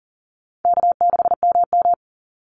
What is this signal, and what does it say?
Morse code, heard as a keyed tone, about 37 wpm, 700 Hz. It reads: K6MM
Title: Call Signs - 40wpm